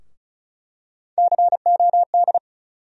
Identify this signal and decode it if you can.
Morse code, heard as a keyed tone, about 36 wpm, 700 Hz. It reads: COD
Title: USA Call Sign Postfixes - 3 Letters 35wpm